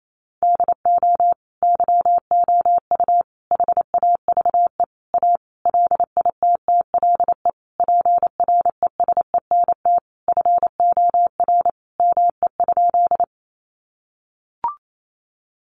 Morse code, heard as a keyed tone, about 28 wpm, 700 Hz. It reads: DO YOU HAVE A LITTLE PRESENT FOR ME?
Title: Sentences from Top 800 Words 28wpm